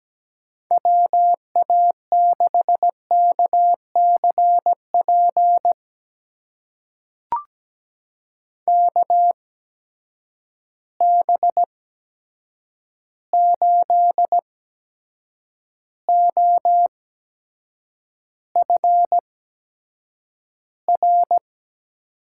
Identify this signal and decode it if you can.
Morse code, heard as a keyed tone, about 17 wpm, spaced as for 7 wpm, 700 Hz. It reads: WA6KCP KB8OFR
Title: USA Call Signs - 6 Characters - 8x Character Spacing - 17wpm